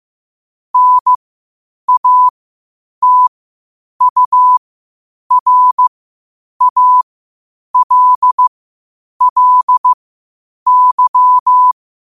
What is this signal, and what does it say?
Morse code, heard as a keyed tone, about 15 wpm, spaced as for 8 wpm, 1000 Hz. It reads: NATURALLY
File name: qotdfiles / 8wpm